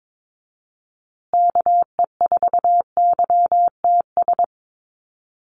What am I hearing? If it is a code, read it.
Morse code, heard as a keyed tone, about 22 wpm, 700 Hz. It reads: KE4YTS
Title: USA Call Signs Repeated 3x 22wpm